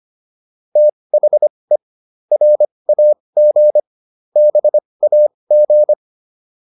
Morse code, heard as a keyed tone, about 25 wpm, spaced as for 20 wpm, 600 Hz. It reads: THE RAG BAG